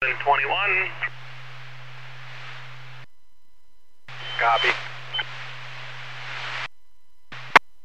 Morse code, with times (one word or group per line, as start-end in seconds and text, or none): none